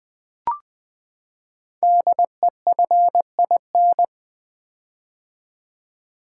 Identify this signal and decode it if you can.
Morse code, heard as a keyed tone, about 20 wpm, 700 Hz. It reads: DEFIN
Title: Build A Word - 5 Letters At A Time - 10-Letter Words - 20wpm